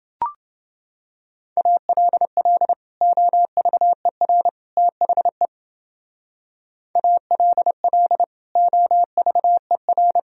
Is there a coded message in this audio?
Yes: Morse code from 1.6 to 10.2 s, heard as a keyed tone, about 29 wpm, 700 Hz. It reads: ALL OVER THE ALL OVER